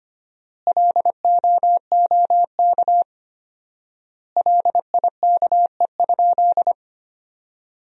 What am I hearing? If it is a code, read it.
Morse code, heard as a keyed tone, about 25 wpm, 700 Hz. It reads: LOOK LIKE?
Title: 4x Word-Spacing - Sentences from Top 100 Words - 25wpm